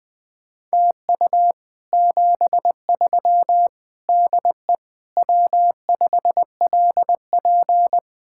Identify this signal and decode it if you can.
Morse code, heard as a keyed tone, about 20 wpm, 700 Hz. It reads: TU 73 DE W5LP
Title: Speed-Racing Common QSO Phrases Encore - 17wpm